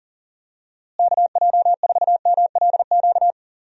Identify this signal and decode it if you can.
Morse code, heard as a keyed tone, about 40 wpm, 700 Hz. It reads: KJ4MLQ
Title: USA Call Signs - 6 Characters - Repeated 3x - 40wpm